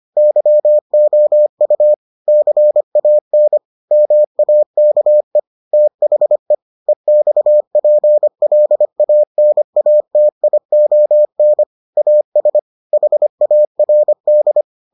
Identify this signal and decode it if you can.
Morse code, heard as a keyed tone, about 25 wpm, 600 Hz. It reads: YOU CAN MAKE THE EXPLANATION AS HARD